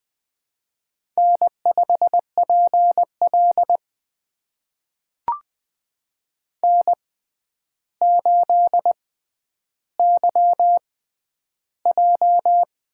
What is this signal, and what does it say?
Morse code, heard as a keyed tone, about 20 wpm, spaced as for 10 wpm, 700 Hz. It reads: N5PL N8YJ
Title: USA Call Signs - 4 Characters - 6x Character Spacing - 20wpm